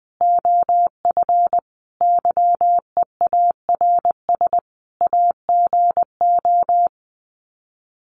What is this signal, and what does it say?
Morse code, heard as a keyed tone, about 20 wpm, 700 Hz. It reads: OF YEARS AGO
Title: Sets of 5 Words 20wpm